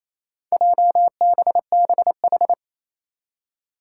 Morse code, heard as a keyed tone, about 27 wpm, 700 Hz. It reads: JBBH
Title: Four Challenging Letter-Number Combinations 28wpm